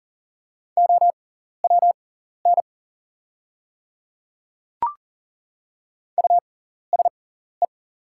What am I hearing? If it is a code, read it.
Morse code, heard as a keyed tone, about 39 wpm, spaced as for 14 wpm, 700 Hz. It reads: OWN USE
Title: 6x Character Spacing - Top 300 Words - 40wpm